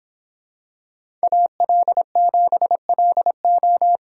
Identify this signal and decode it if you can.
Morse code, heard as a keyed tone, about 25 wpm, 700 Hz. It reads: AL7LO